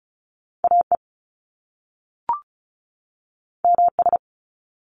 Morse code, heard as a keyed tone, about 36 wpm, 700 Hz. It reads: AE MS
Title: Top Two Letter Combinations in Words 35wpm